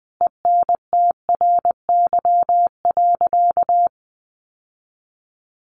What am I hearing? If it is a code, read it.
Morse code, heard as a keyed tone, about 20 wpm, 700 Hz. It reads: ENTRY.
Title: Sentences from Top 1500 Words 20wpm